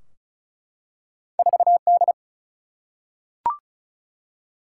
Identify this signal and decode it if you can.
Morse code, heard as a keyed tone, about 34 wpm, 700 Hz. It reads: 4D
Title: Single-Letter Single-Number 35wpm